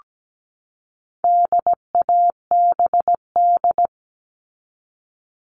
Morse code, heard as a keyed tone, about 17 wpm, 700 Hz. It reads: DABD
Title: All Cut Numbers - 4-Digit 17wpm